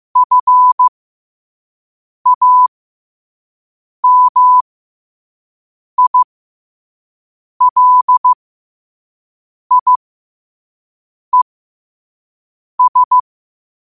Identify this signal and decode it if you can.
Morse code, heard as a keyed tone, about 15 wpm, spaced as for 5 wpm, 1000 Hz. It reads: FAMILIES